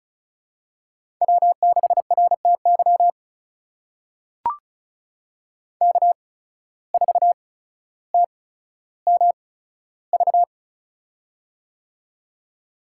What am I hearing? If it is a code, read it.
Morse code, heard as a keyed tone, about 34 wpm, spaced as for 15 wpm, 700 Hz. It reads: W6RTY K4TMV
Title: USA Call Signs - 5 Characters - 8x Character Spacing - 35wpm